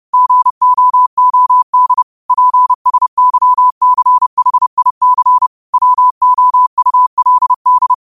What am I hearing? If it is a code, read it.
Morse code, heard as a keyed tone, about 30 wpm, 1000 Hz. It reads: GOOD PSYCHIC WOULD